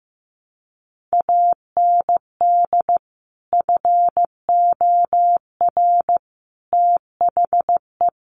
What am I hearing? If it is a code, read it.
Morse code, heard as a keyed tone, about 15 wpm, 700 Hz. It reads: AND FOR THE
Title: Sets of 5 Words 15wpm